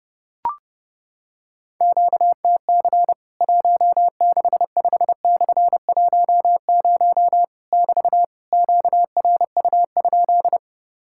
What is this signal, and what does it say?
Morse code, heard as a keyed tone, about 30 wpm, 700 Hz. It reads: QTC 165/10 <BT> QRU?
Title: Worked All Europe DX Contest - QTC Exchange - 30wpm